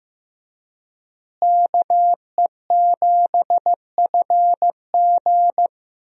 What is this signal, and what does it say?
Morse code, heard as a keyed tone, about 16 wpm, 700 Hz. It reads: KE7FG
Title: USA Call Signs - 5 Characters - Repeated 2x - 15wpm